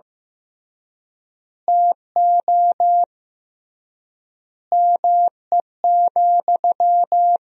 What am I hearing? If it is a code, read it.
Morse code, heard as a keyed tone, about 16 wpm, 700 Hz. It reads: TO ME,
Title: 3x Word-Spacing - Sentences from Top 500 Words - 15wpm